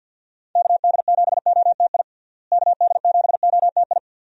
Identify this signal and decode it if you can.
Morse code, heard as a keyed tone, about 46 wpm, 700 Hz. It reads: KD7OTI KD7OTI
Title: USA Call Signs - 6 Characters - Repeated 2x - 50wpm